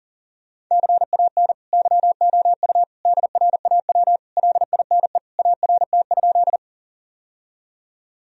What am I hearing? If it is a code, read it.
Morse code, heard as a keyed tone, about 37 wpm, 700 Hz. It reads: CAN YOU DRAW LINE ART?
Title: Sentences from Top 1000 Words - Review Part 3 - 40wpm